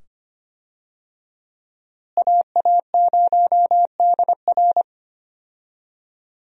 Morse code, heard as a keyed tone, about 25 wpm, 700 Hz. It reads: AA0DR
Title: USA Call Signs - 5 Characters - Repeated 2x - 25wpm